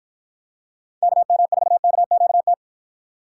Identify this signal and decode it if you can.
Morse code, heard as a keyed tone, about 53 wpm, 700 Hz. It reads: KM4KQT